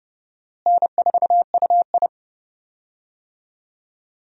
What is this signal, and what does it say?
Morse code, heard as a keyed tone, about 31 wpm, 700 Hz. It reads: N4UI
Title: USA Call Signs - 4 Characters - Repeated 2x - 30wpm